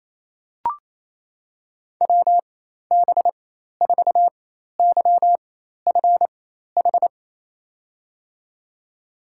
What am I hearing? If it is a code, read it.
Morse code, heard as a keyed tone, about 28 wpm, spaced as for 13 wpm, 700 Hz. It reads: WB4YFH
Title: USA Call Signs - 6 Characters - 4x Character Spacing - 28wpm